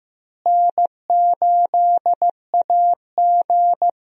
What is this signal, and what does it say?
Morse code, heard as a keyed tone, about 15 wpm, 700 Hz. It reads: N8AG
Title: USA Call Signs - 4 Characters - Repeated 2x - 15wpm